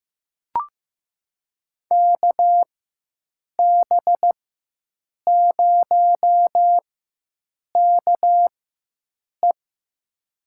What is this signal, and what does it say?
Morse code, heard as a keyed tone, about 16 wpm, spaced as for 7 wpm, 700 Hz. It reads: KB0KE
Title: USA Call Signs - 4x Character Spacing - 15wpm